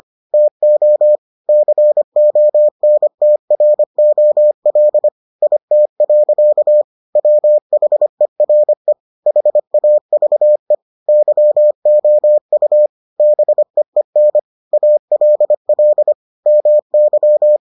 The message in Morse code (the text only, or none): TO CONTROL IT. WHERE HAVE YOU BEEN ALL MY